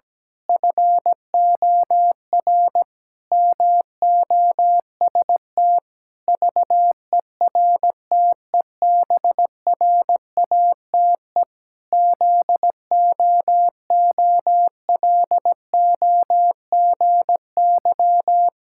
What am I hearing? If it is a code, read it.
Morse code, heard as a keyed tone, about 17 wpm, 700 Hz. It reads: FOR MOST VERTEBRATE ZOOLOGY
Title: Pangram Challenge v2 - 17wpm